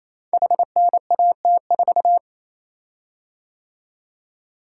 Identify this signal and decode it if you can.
Morse code, heard as a keyed tone, about 27 wpm, 700 Hz. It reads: HNAT4
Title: Code Groups - 5 Letter:Numbers - 28wpm